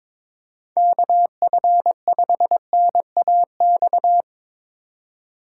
Morse code, heard as a keyed tone, about 22 wpm, 700 Hz. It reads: KF5NAX
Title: USA Call Signs - 6 Characters - 4x Character Spacing - 22wpm